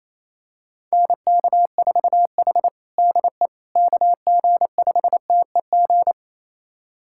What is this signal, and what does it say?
Morse code, heard as a keyed tone, about 28 wpm, 700 Hz. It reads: NK4H DE KG5TEG
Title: USA Callsign DE Callsign 28wpm